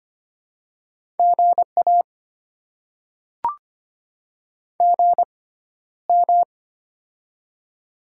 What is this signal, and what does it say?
Morse code, heard as a keyed tone, about 25 wpm, spaced as for 13 wpm, 690 Hz. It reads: GA GM
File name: Top 20 QSO Elements - 6x Character Spacing - 25wpm